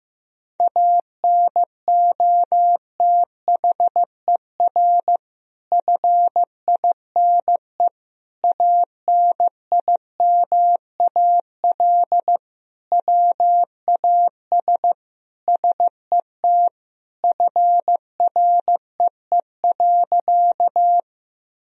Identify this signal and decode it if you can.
Morse code, heard as a keyed tone, about 15 wpm, 700 Hz. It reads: ANOTHER FINE ANIMAL WAS SET FREE.
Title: Sentences from Top 1000 Words - Review Part 1 - 15wpm